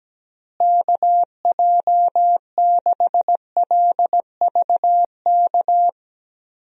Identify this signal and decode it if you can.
Morse code, heard as a keyed tone, about 17 wpm, 700 Hz. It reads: KJ6LVK
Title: USA Call Signs - 6 Characters - Repeated 3x - 17wpm